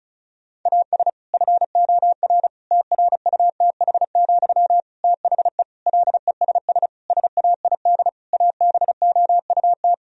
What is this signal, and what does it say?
Morse code, heard as a keyed tone, about 34 wpm, 700 Hz. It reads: AS FOR TRUTH, THE LESS SAID ABOUT
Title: Sentences from Top 1000 Words - Review Part 2 - 35wpm